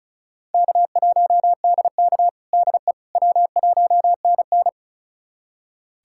Morse code, heard as a keyed tone, about 34 wpm, 700 Hz. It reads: K1DK DE W1NN
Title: USA Callsign DE Callsign - 4 Characters - 35wpm